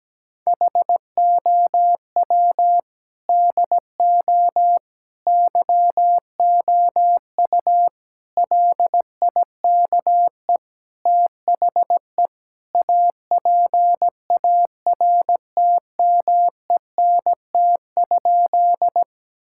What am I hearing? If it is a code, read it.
Morse code, heard as a keyed tone, about 17 wpm, 700 Hz. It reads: HOW DO YOU LIKE THE APARTMENT?
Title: Sentences from Top 1000 Words - Review Part 6 - 17wpm